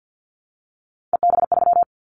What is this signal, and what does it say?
Morse code, heard as a keyed tone, about 50 wpm, 700 Hz. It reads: EB3
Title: Callsign Prefixes for Top 25 DXCC - 50wpm